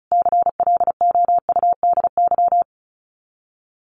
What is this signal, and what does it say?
Morse code, heard as a keyed tone, about 34 wpm, 700 Hz. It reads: CLOUDY
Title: Common QSO Phrases 35wpm